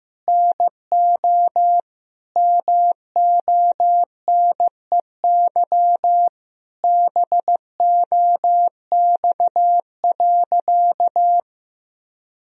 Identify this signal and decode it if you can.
Morse code, heard as a keyed tone, about 15 wpm, 700 Hz. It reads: NO MONEY BOX.
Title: Sentences from Top 1000 Words - Review Part 2 - 15wpm